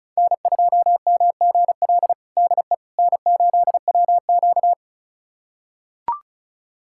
Morse code, heard as a keyed tone, about 36 wpm, 700 Hz. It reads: N2MGL DE N8WQ
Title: USA Callsign DE Callsign 35wpm